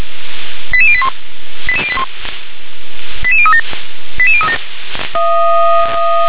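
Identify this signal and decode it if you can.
Five-tone selective call (ZVEI), heard as quick runs of five tones, tones of about 70 ms. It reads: 88081 88081 88037 88037
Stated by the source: SDR recording of a ZVEI five-tone selective call, address 88101.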